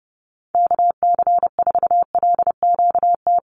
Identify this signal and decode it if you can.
Morse code, heard as a keyed tone, about 30 wpm, 700 Hz. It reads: KC4LQT
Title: USA Call Signs 30wpm